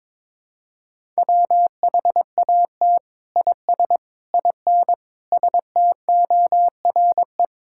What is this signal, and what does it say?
Morse code, heard as a keyed tone, about 22 wpm, 700 Hz. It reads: WHAT IS IN STORE
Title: Sentences from Top 1000 Words - Review Part 5 - 22wpm